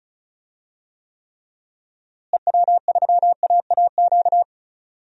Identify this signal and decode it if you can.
Morse code, heard as a keyed tone, about 34 wpm, 700 Hz. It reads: EW3AAQ
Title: DXCC-100 Call Signs - 6 Characters - Repeated 2x - 35wpm